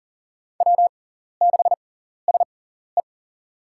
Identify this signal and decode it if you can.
Morse code, heard as a keyed tone, about 40 wpm, spaced as for 14 wpm, 700 Hz. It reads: W6SE
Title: USA Call Signs - 6x Character Spacing - 40wpm